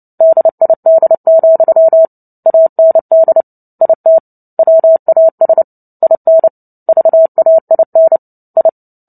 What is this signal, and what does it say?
Morse code, heard as a keyed tone, about 29 wpm, 650 Hz. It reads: DID, AND IT WAS IN VAIN I
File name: Book Two / 03-Chapter 3 - 29 WPM